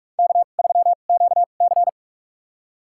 Morse code, heard as a keyed tone, about 45 wpm, spaced as for 32 wpm, 700 Hz. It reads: K3QC
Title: USA Call Signs - 2x Character Spacing - 45wpm